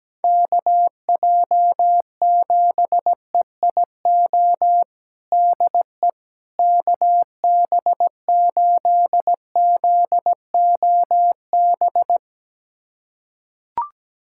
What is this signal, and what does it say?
Morse code, heard as a keyed tone, about 17 wpm, 700 Hz. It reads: KJ7EIO DE KB8ZOB
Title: USA Callsign DE Callsign 17wpm